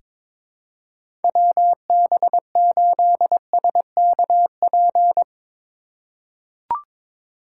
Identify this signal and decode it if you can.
Morse code, heard as a keyed tone, about 22 wpm, 700 Hz. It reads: WB8SKP
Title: K1USN SST QSO Practice - 22wpm